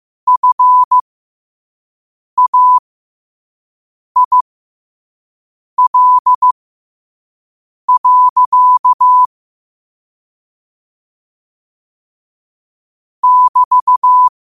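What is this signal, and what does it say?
Morse code, heard as a keyed tone, about 15 wpm, spaced as for 5 wpm, 1000 Hz. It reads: FAIL. <BT>
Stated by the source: Morse practice at 5 wpm.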